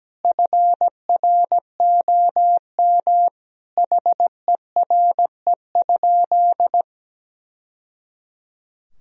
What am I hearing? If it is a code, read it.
Morse code, heard as a keyed tone, about 17 wpm, 700 Hz. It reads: FROM HERE?
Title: Sentences from Top 1000 Words - Review Part 1 - 17wpm